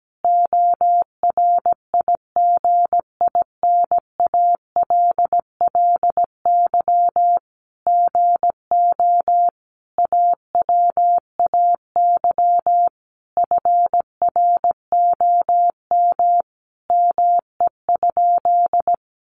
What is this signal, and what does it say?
Morse code, heard as a keyed tone, about 17 wpm, 700 Hz. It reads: ORIGINALLY GO AWAY FROM ME?
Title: Sentences from Top 1500 Words 17wpm